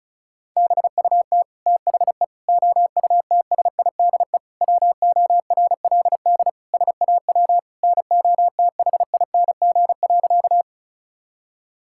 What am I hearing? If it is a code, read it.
Morse code, heard as a keyed tone, about 36 wpm, 700 Hz. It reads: BUT THE OUTSIDE WORLD SAW NOTHING.
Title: Sentences from Top 1000 Words - Review Part 6 - 35wpm